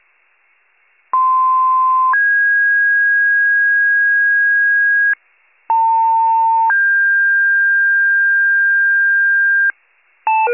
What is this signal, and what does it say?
Two-tone page, heard as a one-second tone, then a three-second tone, about 1040 then 1650 Hz, about 900 then 1600 Hz. It is heard from 1.1 to 9.7 s.